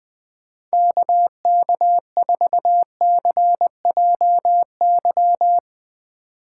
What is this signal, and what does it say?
Morse code, heard as a keyed tone, about 20 wpm, 700 Hz. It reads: KK4CJY
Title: Speed Racing - USA Call Signs 20wpm